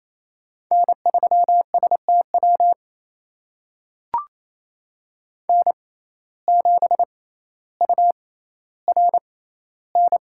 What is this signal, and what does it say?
Morse code, heard as a keyed tone, about 28 wpm, spaced as for 14 wpm, 700 Hz. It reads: N3STW N7URN
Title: USA Call Signs - 6x Character Spacing - 28wpm